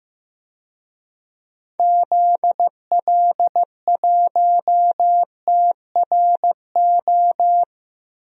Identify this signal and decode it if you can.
Morse code, heard as a keyed tone, about 16 wpm, 700 Hz. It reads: ZL1TRO
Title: DXCC-100 Call Signs - 6 Characters - Repeated 2x - 15wpm